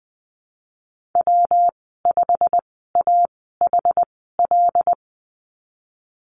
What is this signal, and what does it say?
Morse code, heard as a keyed tone, about 20 wpm, spaced as for 15 wpm, 700 Hz. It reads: W5AHL
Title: 173 Lesson 28g - Number 9 - Call Signs 20wpm